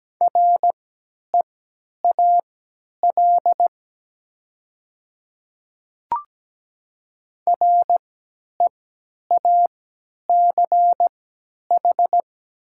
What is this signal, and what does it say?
Morse code, heard as a keyed tone, about 17 wpm, spaced as for 10 wpm, 700 Hz. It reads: REAL REACH
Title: 3x Character Spacing - Top 500 Words - 17wpm